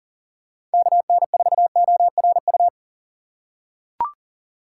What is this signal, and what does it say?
Morse code, heard as a keyed tone, about 40 wpm, 700 Hz. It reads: KN4ORU
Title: USA Call Signs - 6 Characters - 2x Character Spacing - 40wpm